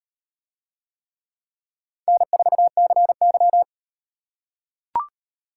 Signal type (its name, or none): Morse code